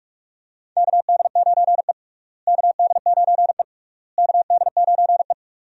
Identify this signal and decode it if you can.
Morse code, heard as a keyed tone, about 46 wpm, 700 Hz. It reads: KD9E KD9E KD9E
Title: USA Call Signs - 4 Characters - Repeated 3x - 45wpm